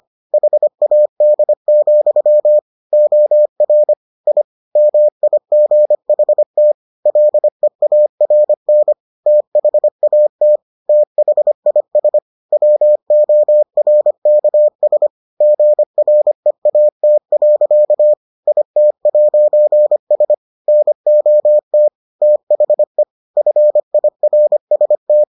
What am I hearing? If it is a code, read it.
Morse code, heard as a keyed tone, about 25 wpm, 600 Hz. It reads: HAD, OR I MIGHT LEARN THAT THIS WORKS GREAT. IT'S NOT THE FIRST